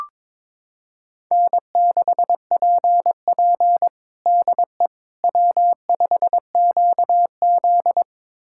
Morse code, heard as a keyed tone, about 22 wpm, 700 Hz. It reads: N6PP DE W5QZ
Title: USA Callsign DE Callsign - 4 Characters - 22wpm